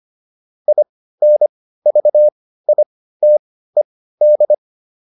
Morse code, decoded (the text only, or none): INVITED